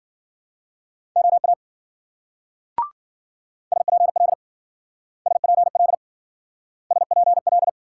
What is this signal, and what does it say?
Morse code, heard as a keyed tone, about 53 wpm, 700 Hz. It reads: OA SPL SPL SPL